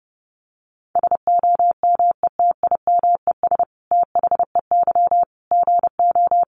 Morse code, heard as a keyed tone, about 30 wpm, 700 Hz. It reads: SOMETIMES THEY GO
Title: Sentences from Top 600 Words 30wpm